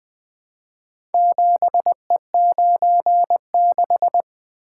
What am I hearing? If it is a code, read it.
Morse code, heard as a keyed tone, about 20 wpm, 700 Hz. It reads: 7E96
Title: Four Letter-Number Combinations - 20wpm